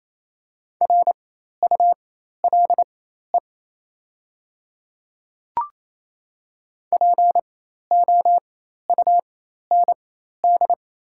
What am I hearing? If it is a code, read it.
Morse code, heard as a keyed tone, about 28 wpm, spaced as for 13 wpm, 690 Hz. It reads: RULE POUND
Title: 4x Character Spacing - Top 500 Words - 28wpm